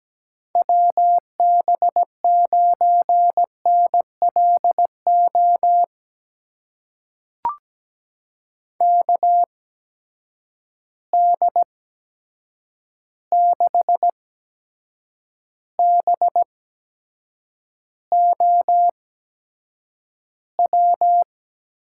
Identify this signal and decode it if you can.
Morse code, heard as a keyed tone, about 17 wpm, spaced as for 7 wpm, 700 Hz. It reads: WB9NLO KD6BOW
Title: USA Call Signs - 8x Character Spacing - 17wpm